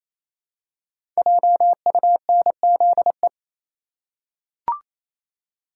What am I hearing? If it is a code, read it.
Morse code, heard as a keyed tone, about 29 wpm, 700 Hz. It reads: JUNZE
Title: Code Groups - 5 Letters - 28wpm